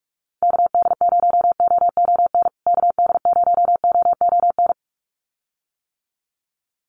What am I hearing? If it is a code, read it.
Morse code, heard as a keyed tone, about 44 wpm, 700 Hz. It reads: KD0OON KD0OON
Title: USA Call Signs Repeated 2x 45wpm